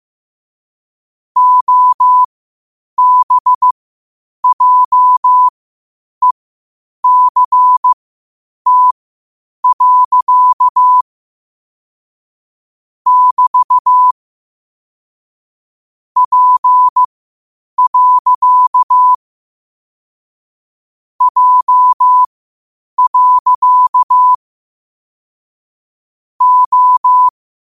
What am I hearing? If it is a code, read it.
Morse code, heard as a keyed tone, about 15 wpm, spaced as for 8 wpm, 1000 Hz. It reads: OBJECT. <BT> P. J. O